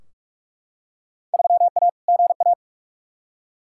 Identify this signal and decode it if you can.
Morse code, heard as a keyed tone, about 41 wpm, 700 Hz. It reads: 3A GA
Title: Field Day Exchanges - 45wpm